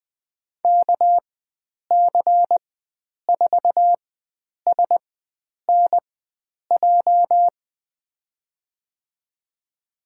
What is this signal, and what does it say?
Morse code, heard as a keyed tone, about 20 wpm, spaced as for 9 wpm, 700 Hz. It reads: KC4SNJ
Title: USA Call Signs - 6 Characters - 4x Character Spacing - 20wpm